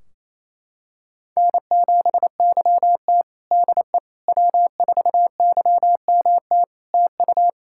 Morse code, heard as a keyed tone, about 28 wpm, 700 Hz. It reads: N7YT DE W4YMT TU
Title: K1USN SST QSO Practice 28wpm